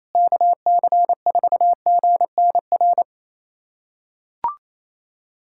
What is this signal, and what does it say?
Morse code, heard as a keyed tone, about 27 wpm, 700 Hz. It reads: KC4GNR
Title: USA Callsign DE Callsign - 6 Characters - 28wpm